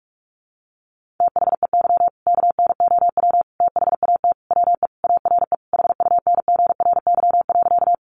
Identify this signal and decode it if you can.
Morse code, heard as a keyed tone, about 44 wpm, 700 Hz. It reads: THEY KNOW THAT WE ARE HUNGRY.
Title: Sentences from Top 1400 Words 45wpm